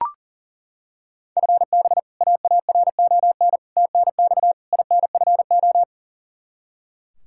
Morse code, heard as a keyed tone, about 40 wpm, 700 Hz. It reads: FB AARON TNX INFO
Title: K1USN SST QSO Practice 40wpm